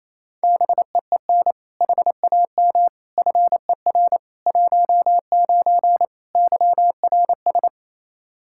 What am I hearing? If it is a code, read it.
Morse code, heard as a keyed tone, about 28 wpm, 700 Hz. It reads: BEEN HAM FER 19 YRS
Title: Practice QSO - Exchange 3 - More Info - 28wpm